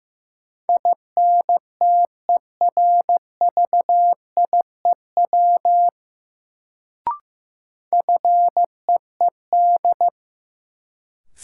Morse code, heard as a keyed tone, about 15 wpm, 700 Hz. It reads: INTERVIEW FEED
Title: Sentences from Top 1600 Words - New Word Drill Alt - 15wpm